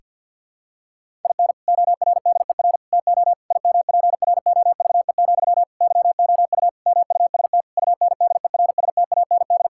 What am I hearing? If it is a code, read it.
Morse code, heard as a keyed tone, about 50 wpm, 700 Hz. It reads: IN ORDER TO IMPROVE, YOU MUST UNDERSTAND
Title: Sentences from Top 1000 Words - Review Part 5 - 50wpm